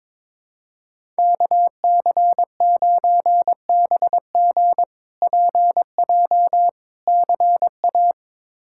Morse code, heard as a keyed tone, about 22 wpm, 700 Hz. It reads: KC9BG PJ CA
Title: K1USN SST QSO Practice - 22wpm